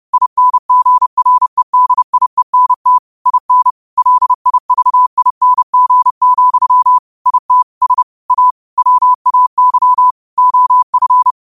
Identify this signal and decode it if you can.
Morse code, heard as a keyed tone, about 30 wpm, 1000 Hz. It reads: INGREDIENT IN LIVING, IT S A WAY OF